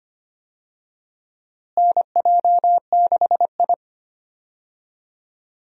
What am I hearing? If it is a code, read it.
Morse code, heard as a keyed tone, about 25 wpm, 700 Hz. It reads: NJ6I
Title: USA Call Signs - 4 Characters - Repeated 2x - 25wpm